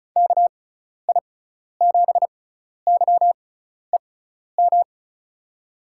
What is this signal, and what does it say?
Morse code, heard as a keyed tone, about 36 wpm, spaced as for 12 wpm, 700 Hz. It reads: KI7YEM
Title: USA Call Signs - 6 Characters - 6x Character Spacing - 35wpm